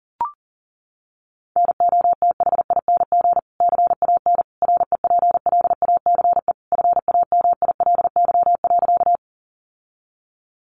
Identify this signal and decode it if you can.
Morse code, heard as a keyed tone, about 40 wpm, 700 Hz. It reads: NOTHING CAN REPLACE FAMILY.
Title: Sentences from Top 1400 Words 40wpm